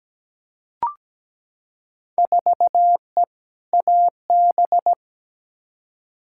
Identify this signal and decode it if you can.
Morse code, heard as a keyed tone, about 17 wpm, 700 Hz. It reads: 4E AB
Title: Field Day Exchanges Encore - 17wpm